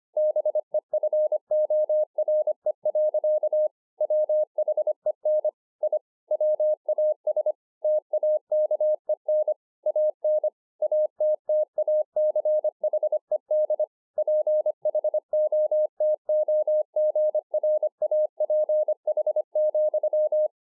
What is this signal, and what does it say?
Morse code, heard as a keyed tone, about 25 wpm, 610 Hz. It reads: BEFORE. WHEN I WAS TAKEN AN ATTACHED PHOTOGRAPH,